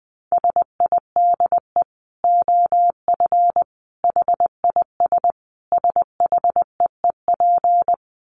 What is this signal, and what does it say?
Morse code, heard as a keyed tone, about 20 wpm, 700 Hz. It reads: SIDE OF HIS SHEEP
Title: Dit Challenge 20wpm